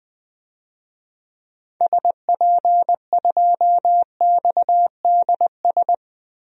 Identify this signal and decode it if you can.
Morse code, heard as a keyed tone, about 20 wpm, 700 Hz. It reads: SP2XDS
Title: DXCC-100 Call Signs - 6 Characters - Repeated 2x - 20wpm